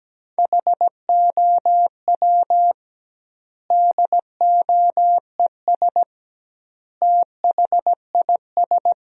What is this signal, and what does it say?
Morse code, heard as a keyed tone, about 17 wpm, 700 Hz. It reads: HOW DOES THIS
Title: 2x Word-Spacing - Sentences from Top 400 Words - 17wpm